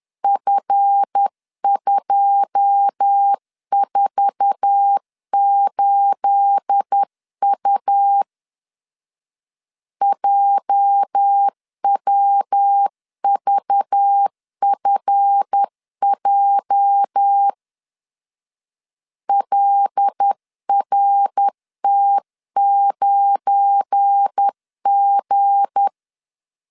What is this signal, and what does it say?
Morse code, heard as a keyed tone, about 10 wpm, 800 Hz. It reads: F248U JWVFJ LRT9G
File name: LEZIONI / 9 28 10wpm 2842